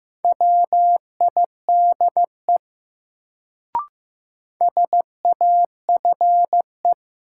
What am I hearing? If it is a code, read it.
Morse code, heard as a keyed tone, about 15 wpm, 700 Hz. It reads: WIDE SAFE
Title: Sentences from Top 900 Words - New Word Drill - 15wpm